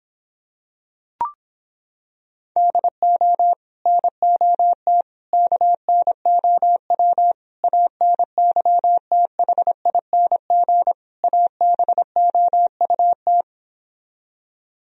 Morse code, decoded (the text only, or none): DO NOT KNOW ANYTHING ABOUT